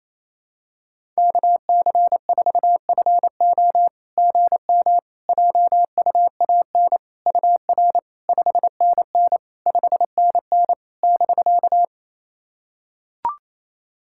Morse code, heard as a keyed tone, about 28 wpm, 700 Hz. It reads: KC4FO GM JUAN UR 5NN 5NN <BK>
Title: SOTA QSO Practice 28wpm